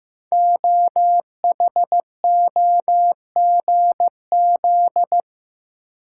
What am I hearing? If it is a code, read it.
Morse code, heard as a keyed tone, about 15 wpm, 700 Hz. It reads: OHOGZ
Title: Code Groups - 5 Letters - 15wpm